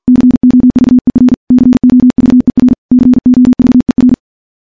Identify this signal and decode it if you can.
Morse code, heard as a keyed tone, about 51 wpm, 250 Hz. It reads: COVER COVER COVER
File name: morse / morse-training-50WPM